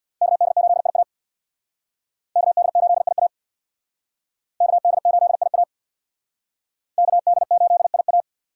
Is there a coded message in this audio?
Yes: Morse code from 0.2 to 8.2 s, heard as a keyed tone, about 63 wpm, 700 Hz. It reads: KD8IA KD8IA KD8IA KD8IA